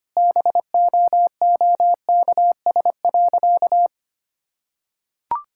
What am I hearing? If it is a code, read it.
Morse code, heard as a keyed tone, about 25 wpm, 700 Hz. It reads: BOOKS.
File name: practice / 25wpm / Sentences with 2-Word Phrases - 25wpm